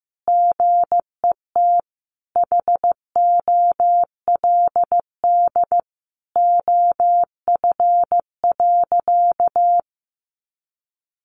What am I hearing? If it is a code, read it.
Morse code, heard as a keyed tone, about 15 wpm, 700 Hz. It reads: GET HOLD OF.
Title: A Sound Of Thunder 15wpm